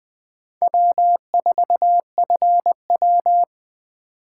Morse code, heard as a keyed tone, about 21 wpm, 700 Hz. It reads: W4FW